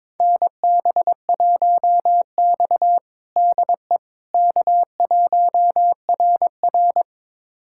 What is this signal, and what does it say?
Morse code, heard as a keyed tone, about 22 wpm, 700 Hz. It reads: NB1X DE K1RR
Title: USA Callsign DE Callsign - 4 Characters - 22wpm